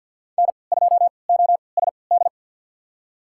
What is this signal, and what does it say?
Morse code, heard as a keyed tone, about 50 wpm, spaced as for 28 wpm, 700 Hz. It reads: N2OSD
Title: USA Call Signs - 5 Characters - 3x Character Spacing - 50wpm